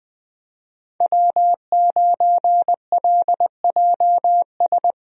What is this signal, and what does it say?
Morse code, heard as a keyed tone, about 20 wpm, 700 Hz. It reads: W9LJS
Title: USA Call Signs - 5 Characters - Repeated 3x - 20wpm